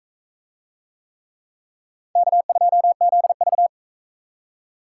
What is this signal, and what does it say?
Morse code, heard as a keyed tone, about 44 wpm, 700 Hz. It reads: K2ZV